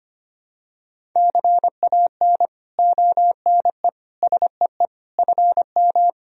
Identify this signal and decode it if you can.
Morse code, heard as a keyed tone, about 25 wpm, 700 Hz. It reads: CAN ONE SEE FM
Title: Sentences from Top 100 Words and Abbreviations 25wpm